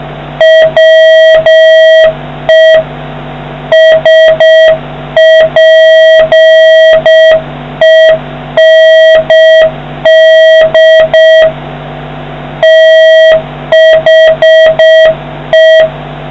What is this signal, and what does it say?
Morse code, heard as a keyed tone, about 6 wpm, 1900 Hz, with noise. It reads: WE SPEND THE